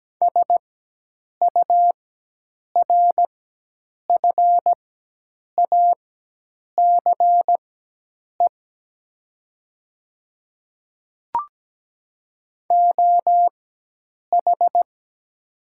Morse code, heard as a keyed tone, about 17 wpm, spaced as for 8 wpm, 700 Hz. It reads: SURFACE OH
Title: 4x Character Spacing - Top 500 Words - 17wpm